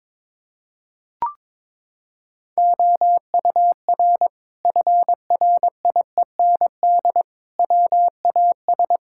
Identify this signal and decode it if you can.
Morse code, heard as a keyed tone, about 22 wpm, 700 Hz. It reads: OUR FRIEND WAS
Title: Sentences from Top 1000 Words - Review Part 5 - 22wpm